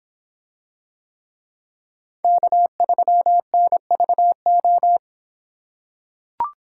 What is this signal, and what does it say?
Morse code, heard as a keyed tone, about 25 wpm, 700 Hz. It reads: K3NVO